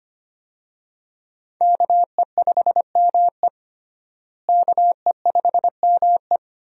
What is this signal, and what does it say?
Morse code, heard as a keyed tone, about 25 wpm, 700 Hz. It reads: KE5ME KE5ME
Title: USA Call Signs - 5 Characters - Repeated 3x - 25wpm